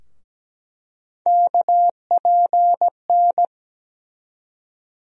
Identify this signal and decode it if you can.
Morse code, heard as a keyed tone, about 17 wpm, 700 Hz. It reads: KPN
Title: USA Call Sign Postfixes - 3 Letters 17wpm